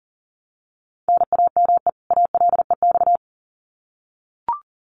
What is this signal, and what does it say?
Morse code, heard as a keyed tone, about 40 wpm, 700 Hz. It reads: NAME ALEX
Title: Common QSO Phrases 40wpm